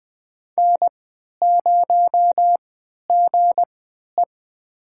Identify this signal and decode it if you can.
Morse code, heard as a keyed tone, about 20 wpm, spaced as for 11 wpm, 700 Hz. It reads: N0GE
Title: USA Call Signs - 4 Characters - 3x Character Spacing - 20wpm